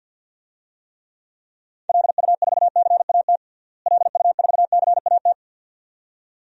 Morse code, heard as a keyed tone, about 48 wpm, 700 Hz. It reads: LU4CAT LU4CAT
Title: DXCC-100 Call Signs - 6 Characters - Repeated 2x - 50wpm